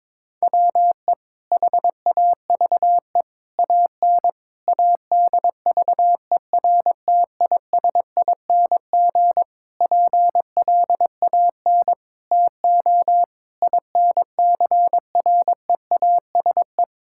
Morse code, heard as a keyed tone, about 22 wpm, 700 Hz. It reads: WE HAVE AN ADVERTISING PLAN TO INCREASE
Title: Sentences from Top 1000 Words - Review Part 6 - 22wpm